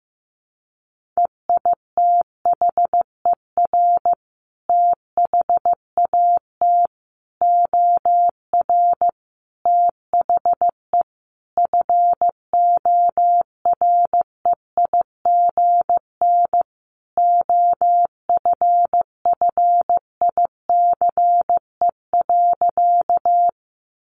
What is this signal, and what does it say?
Morse code, heard as a keyed tone, about 15 wpm, 700 Hz. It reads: EITHER THAT OR THE FOREIGN OFFICE.
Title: Sentences from Top 1400 Words 15wpm